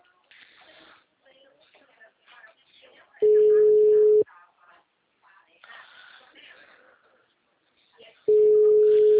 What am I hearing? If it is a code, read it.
Ringback tone (Europe, 425 Hz), with speech.
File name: out- -2003- - - .36